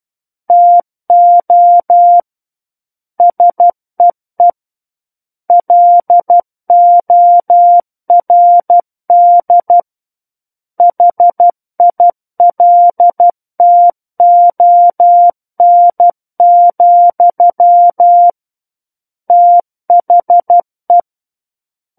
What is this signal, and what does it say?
Morse code, heard as a keyed tone, about 12 wpm, 700 Hz. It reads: TO SEE LORD HILTON, THE
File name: Book One / 03-Chapter 3 - 12 WPM